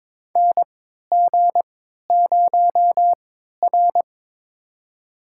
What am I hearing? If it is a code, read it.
Morse code, heard as a keyed tone, about 22 wpm, spaced as for 13 wpm, 700 Hz. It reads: NG0R
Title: USA Call Signs - 4 Characters - 3x Character Spacing - 22wpm